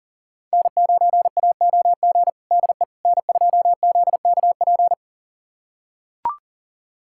Morse code, heard as a keyed tone, about 40 wpm, 700 Hz. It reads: N9AOG DE N2ZKP
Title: USA Callsign DE Callsign - 5 Characters - 40wpm